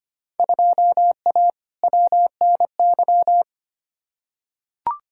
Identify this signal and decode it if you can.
Morse code, heard as a keyed tone, about 25 wpm, 700 Hz. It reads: 2A WNY
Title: Field Day Exchanges - 25wpm